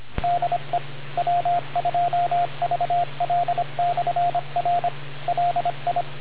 Morse code, heard as a keyed tone, about 25 wpm, 700 Hz, with radio noise. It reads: DE W2VL/R LI